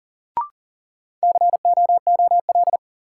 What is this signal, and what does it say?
Morse code, heard as a keyed tone, about 40 wpm, 700 Hz. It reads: COOL
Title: Sentences from Top 1200 Words - New Word Drill Alt - 40wpm